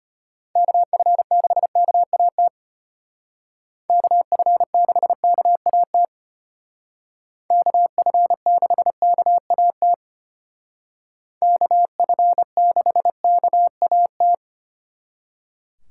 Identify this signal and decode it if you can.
Morse code, heard as a keyed tone, about 33 wpm, 700 Hz. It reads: KF6KAT KF6KAT KF6KAT KF6KAT